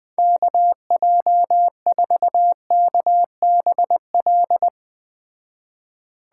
Morse code, heard as a keyed tone, about 20 wpm, 700 Hz. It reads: KJ4KBL
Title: Speed Racing - USA 2x3 Call Signs 17wpm